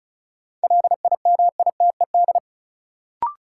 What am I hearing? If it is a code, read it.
Morse code, heard as a keyed tone, about 34 wpm, 700 Hz. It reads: LIMITED
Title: Sentences from Top 900 Words - New Word Drill - 35wpm